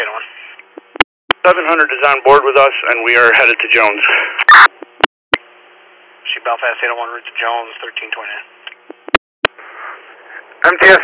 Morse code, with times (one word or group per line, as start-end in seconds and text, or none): none